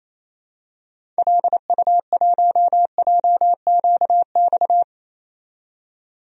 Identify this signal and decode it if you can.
Morse code, heard as a keyed tone, about 27 wpm, 700 Hz. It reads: LU1JQX
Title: DXCC-100 Call Signs - 6 Characters - 28wpm